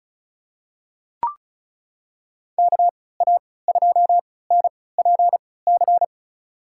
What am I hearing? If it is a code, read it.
Morse code, heard as a keyed tone, about 34 wpm, spaced as for 20 wpm, 700 Hz. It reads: KA2NPC
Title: USA Call Signs - 6 Characters - 3x Character Spacing - 35wpm